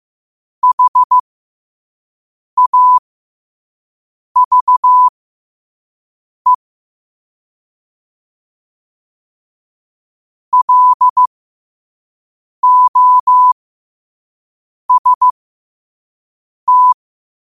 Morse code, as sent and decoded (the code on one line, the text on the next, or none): .... .- ...- . / .-.. --- ... -
HAVE LOST